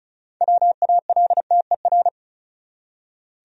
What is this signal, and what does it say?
Morse code, heard as a keyed tone, about 34 wpm, 700 Hz. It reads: WALTER
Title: USA Common Names - 35wpm